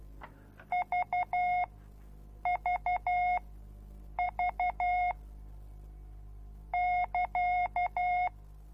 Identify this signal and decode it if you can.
Morse code, heard as a keyed tone, about 12 wpm, spaced as for 7 wpm, 720 Hz. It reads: VVV <KA>